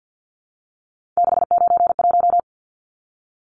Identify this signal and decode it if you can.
Morse code, heard as a keyed tone, about 50 wpm, 700 Hz. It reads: 691
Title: Numbers - 3-Digit 50wpm